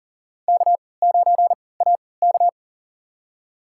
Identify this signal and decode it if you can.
Morse code, heard as a keyed tone, about 40 wpm, spaced as for 23 wpm, 700 Hz. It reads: K9AK
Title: USA Call Signs - 3x Character Spacing - 40wpm